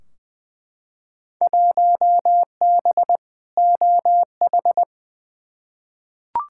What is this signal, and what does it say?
Morse code, heard as a keyed tone, about 20 wpm, 700 Hz. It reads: 1B OH
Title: Field Day Exchanges - 20wpm